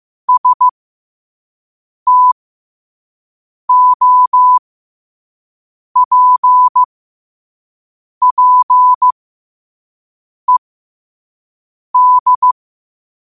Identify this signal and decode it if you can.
Morse code, heard as a keyed tone, about 15 wpm, spaced as for 5 wpm, 1000 Hz. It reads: STOPPED
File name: qotdfiles / 5wpm